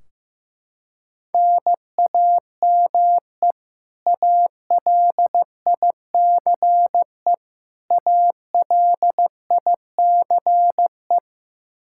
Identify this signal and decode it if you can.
Morse code, heard as a keyed tone, about 15 wpm, 700 Hz. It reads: NAME ALICE ALICE
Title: Practice QSO - Exchange 1 - Beginning - 15wpm